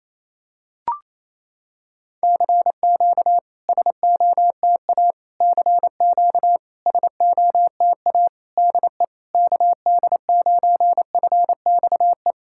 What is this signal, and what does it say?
Morse code, heard as a keyed tone, about 28 wpm, 700 Hz. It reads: CQ SOTA CQ SOTA DE KD9FXE
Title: SOTA QSO Practice 28wpm